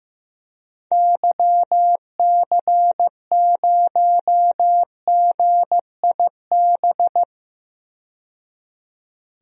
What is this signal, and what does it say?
Morse code, heard as a keyed tone, about 15 wpm, 700 Hz. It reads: YC0GIB
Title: DXCC-100 Call Signs - 6 Characters - 15wpm